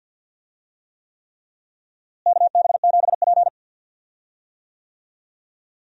Morse code, heard as a keyed tone, about 53 wpm, 700 Hz. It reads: KB7P
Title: USA Call Signs - 4 Characters - Repeated 3x - 50wpm